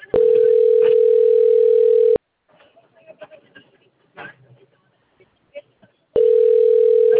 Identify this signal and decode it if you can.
Ringback tone (North America), 440 + 480 Hz, with speech.
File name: rg-1600-unknown- - - .7825